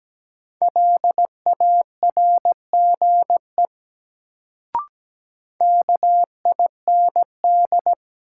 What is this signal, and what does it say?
Morse code, heard as a keyed tone, about 17 wpm, 700 Hz. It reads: LARGE KIND
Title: Sentences from Top 200 Words - New Word Drill - 17wpm